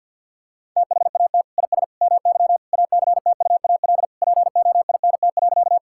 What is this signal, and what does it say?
Morse code, heard as a keyed tone, about 50 wpm, 700 Hz. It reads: THAT IS MY ACTUAL POINT.
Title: Sentences from Top 1000 Words - Review Part 5 - 50wpm